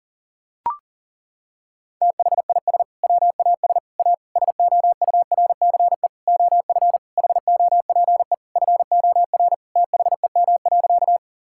Morse code, heard as a keyed tone, about 40 wpm, 700 Hz. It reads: THIS WAS A SOURCE OF HOPE FOR THEM.
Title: Sentences from Top 1000 Words - Review Part 5 - 40wpm